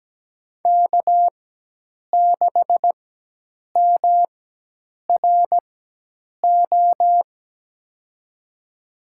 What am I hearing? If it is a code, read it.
Morse code, heard as a keyed tone, about 17 wpm, spaced as for 8 wpm, 700 Hz. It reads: K6MRO
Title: USA Call Signs - 5 Characters - 4x Character Spacing - 17wpm